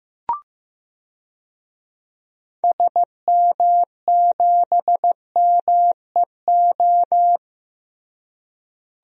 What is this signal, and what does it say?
Morse code, heard as a keyed tone, about 15 wpm, 700 Hz. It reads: SM7MEO
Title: DXCC-100 Call Signs - 6 Characters - Repeated 3x - 15wpm